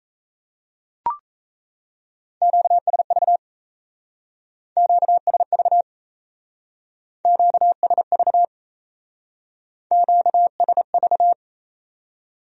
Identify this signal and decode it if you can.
Morse code, heard as a keyed tone, about 37 wpm, 700 Hz. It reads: QSV QSV QSV QSV